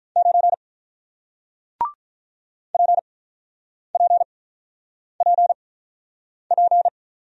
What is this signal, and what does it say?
Morse code, heard as a keyed tone, about 46 wpm, spaced as for 9 wpm, 700 Hz. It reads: 9 PPPP